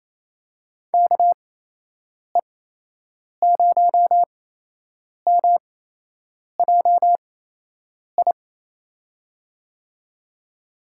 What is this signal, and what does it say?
Morse code, heard as a keyed tone, about 28 wpm, spaced as for 8 wpm, 700 Hz. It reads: KE0MJI
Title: USA Call Signs - 6 Characters - 8x Character Spacing - 28wpm